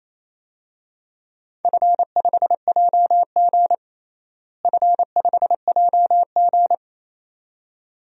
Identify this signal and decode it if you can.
Morse code, heard as a keyed tone, about 28 wpm, 700 Hz. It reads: F5JG F5JG
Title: DXCC-100 Call Signs - 4 Characters - Repeated 2x - 28wpm